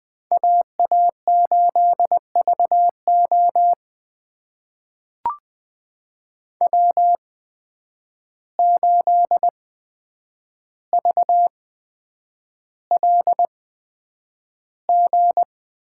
Morse code, heard as a keyed tone, about 20 wpm, spaced as for 9 wpm, 700 Hz. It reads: AA8VO W8VLG
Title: USA Call Signs - 5 Characters - 8x Character Spacing - 20wpm